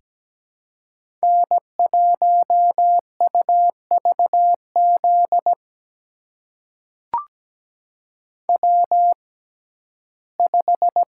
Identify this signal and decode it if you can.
Morse code, heard as a keyed tone, about 17 wpm, spaced as for 12 wpm, 700 Hz. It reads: N1UVZ W5
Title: USA Call Signs - 5 Characters - 6x Character Spacing - 17wpm